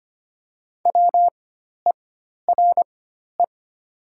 Morse code, heard as a keyed tone, about 25 wpm, spaced as for 12 wpm, 700 Hz. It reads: WERE
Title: 4x Character Spacing - Top 300 Words - 25wpm